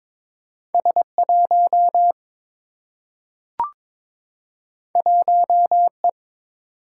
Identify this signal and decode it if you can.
Morse code, heard as a keyed tone, about 22 wpm, 700 Hz. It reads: S1 1E
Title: Single-Letter Single-Number 22wpm